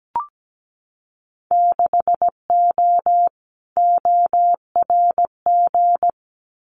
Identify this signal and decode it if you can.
Morse code, heard as a keyed tone, about 17 wpm, 700 Hz. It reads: 6O ORG
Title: Winter Field Day Exchanges Encore 17wpm